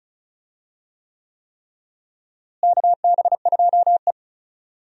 Morse code, heard as a keyed tone, about 34 wpm, 700 Hz. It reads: KB2E
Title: USA Call Signs - 4 Characters - Repeated 3x - 35wpm